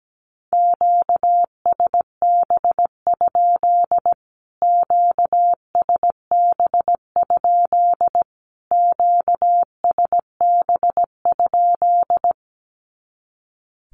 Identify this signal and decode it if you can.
Morse code, heard as a keyed tone, about 17 wpm, 700 Hz. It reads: QSB? QSB? QSB?
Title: Q-Codes Repeated 3x 17wpm